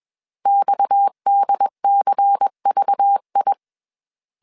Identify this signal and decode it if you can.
Morse code, heard as a keyed tone, about 21 wpm, 800 Hz. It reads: XDCVI
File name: CW / DET / 1 6 1500C S C 20wpm 2116